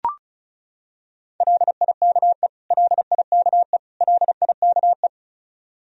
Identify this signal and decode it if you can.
Morse code, heard as a keyed tone, about 34 wpm, 700 Hz. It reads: LIKE LIKE LIKE
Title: Top 100 Words Repeated 35wpm